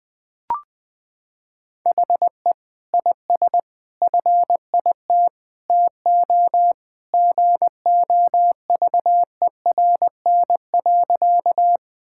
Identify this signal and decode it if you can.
Morse code, heard as a keyed tone, about 20 wpm, 700 Hz. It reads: HE IS FIT TO GOVERN.
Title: Sentences from Top 1000 Words - Review Part 5 - 20wpm